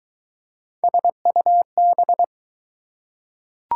Morse code, heard as a keyed tone, about 20 wpm, 700 Hz. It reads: SUB